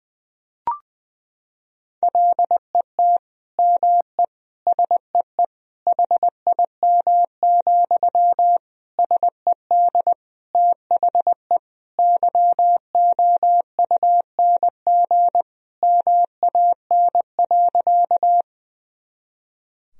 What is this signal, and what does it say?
Morse code, heard as a keyed tone, about 20 wpm, 700 Hz. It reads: LET ME SEE HIM, SED THE YOUNG MAN.
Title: Sentences from Top 400 Words and Abbreviations 20wpm